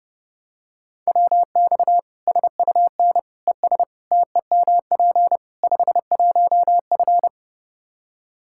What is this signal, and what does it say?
Morse code, heard as a keyed tone, about 32 wpm, 700 Hz. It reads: WX SUN ES TEMP 51F
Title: Practice QSO - Exchange 2 - Info - 30wpm